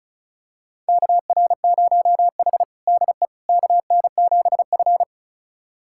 Morse code, heard as a keyed tone, about 36 wpm, 690 Hz. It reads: KR0H DE KN7F
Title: USA Callsign DE Callsign - 4 Characters - 35wpm